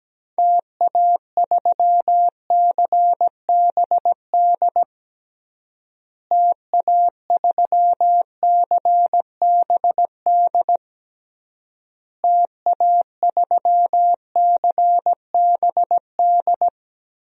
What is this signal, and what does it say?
Morse code, heard as a keyed tone, about 17 wpm, 700 Hz. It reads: TA3CBD TA3CBD TA3CBD
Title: DXCC-100 Call Signs - 6 Characters - Repeated 3x - 17wpm